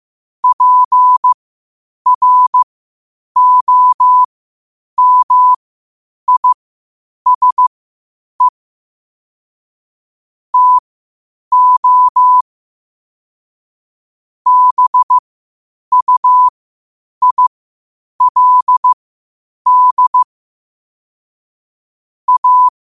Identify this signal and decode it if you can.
Morse code, heard as a keyed tone, about 15 wpm, spaced as for 8 wpm, 1000 Hz. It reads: PROMISE TO BUILD A
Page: New quotes every day in morse code at 8 Words per minute.